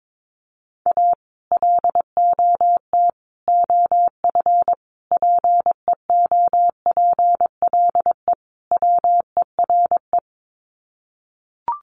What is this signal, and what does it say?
Morse code, heard as a keyed tone, about 22 wpm, 700 Hz. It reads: A LOT OF PEOPLE WERE
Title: Sets of 5 Words Encore2 22wpm